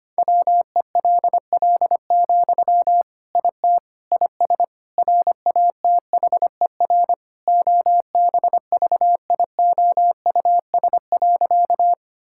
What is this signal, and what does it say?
Morse code, heard as a keyed tone, about 25 wpm, 700 Hz. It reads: WELL, IT IS RATHER OBVIOUS.
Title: Sentences from Top 1600 Words 25wpm